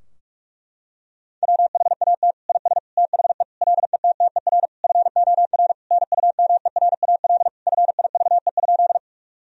Morse code, heard as a keyed tone, about 41 wpm, 700 Hz. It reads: WHAT IS THE LETTER FOR NUMERAL FIVE?
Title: Sentences from Top 1000 Words - Review Part 2 - 45wpm